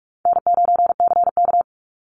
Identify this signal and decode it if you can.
Morse code, heard as a keyed tone, about 46 wpm, 700 Hz. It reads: N9CK
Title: Easy Call Signs - 45wpm